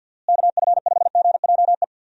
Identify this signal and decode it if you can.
Morse code, heard as a keyed tone, about 50 wpm, 700 Hz. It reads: KF5GJE
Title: USA Call Signs - 6 Characters - Repeated 3x - 50wpm